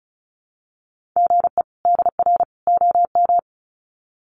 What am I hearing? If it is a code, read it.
Morse code, heard as a keyed tone, about 34 wpm, 700 Hz. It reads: GE DR OM
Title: Common QSO Phrases 35wpm